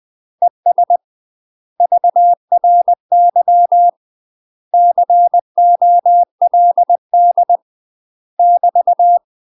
Morse code, heard as a keyed tone, about 20 wpm, 700 Hz. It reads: ES VRY COLD <BT>
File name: UK MASTER QSOs TRAINING / 20WPM / UK MASTER QSOs 0052